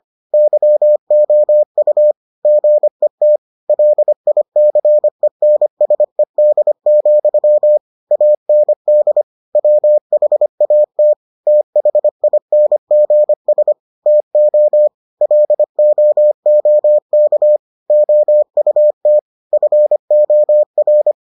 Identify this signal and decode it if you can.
Morse code, heard as a keyed tone, about 25 wpm, 600 Hz. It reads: YOU GET LICENSED, AND WHAT THINGS TO LOOK OUT FOR